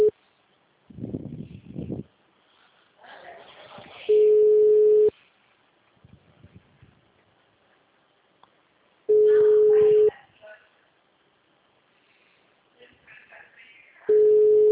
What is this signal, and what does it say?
Ringback tone (Europe, 425 Hz), with speech.